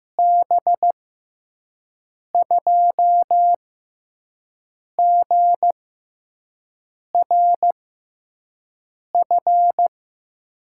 Morse code, heard as a keyed tone, about 15 wpm, spaced as for 5 wpm, 700 Hz. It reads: B2GRF
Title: USA Call Signs - 6 Characters - 6x Character Spacing - 15wpm